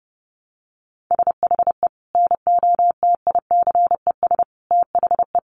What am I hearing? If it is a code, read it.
Morse code, heard as a keyed tone, about 30 wpm, 700 Hz. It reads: SHE NOTICES THE
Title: Easy Phrases 30wpm